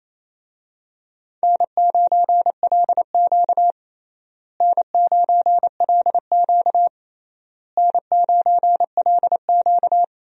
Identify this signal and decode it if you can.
Morse code, heard as a keyed tone, about 28 wpm, 700 Hz. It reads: N9LQ N9LQ N9LQ
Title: USA Call Signs - 4 Characters - Repeated 3x - 28wpm